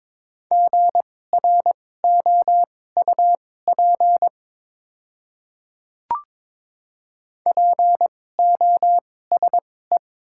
Morse code, heard as a keyed tone, about 22 wpm, spaced as for 16 wpm, 700 Hz. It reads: GROUP POSE
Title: 2x Character Spacing - Top 500 Words - 22wpm